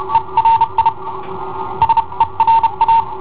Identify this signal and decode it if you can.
Morse code, heard as a keyed tone, about 30 wpm, spaced as for 26 wpm, 910 Hz. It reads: ERI SERA
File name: i7ohp / cassetta i7ohp B